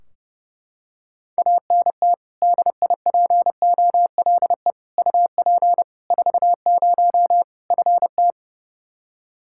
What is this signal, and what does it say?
Morse code, heard as a keyed tone, about 30 wpm, 700 Hz. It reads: ANT DIPOLE UP 40 FT
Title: Practice QSO - Exchange 2 - Info - 30wpm